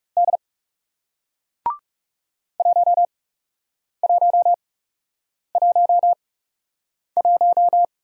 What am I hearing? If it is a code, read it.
Morse code, heard as a keyed tone, about 41 wpm, spaced as for 8 wpm, 700 Hz. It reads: D 1111